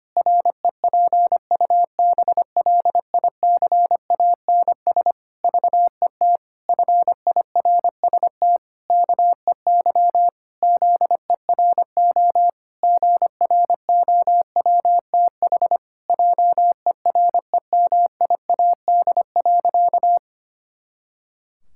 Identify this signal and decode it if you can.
Morse code, heard as a keyed tone, about 25 wpm, 700 Hz. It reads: REPUBLICANS VET FIRST KEY ZERO GROWTH JEREMIAD.
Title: Pangram Challenge v2 - 25wpm